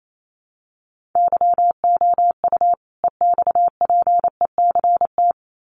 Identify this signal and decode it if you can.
Morse code, heard as a keyed tone, about 28 wpm, 700 Hz. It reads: YOU EXPECT
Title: Sentences from Top 1100 Words 28wpm